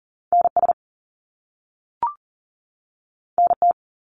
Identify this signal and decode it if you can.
Morse code, heard as a keyed tone, about 38 wpm, 700 Hz. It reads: NS NT
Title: Canadian Provinces and Territories 40wpm